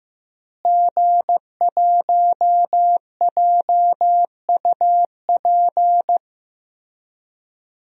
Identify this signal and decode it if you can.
Morse code, heard as a keyed tone, about 15 wpm, 700 Hz. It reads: G1JUP
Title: DXCC-100 Call Signs - 5 Characters - Repeated 3x - 15wpm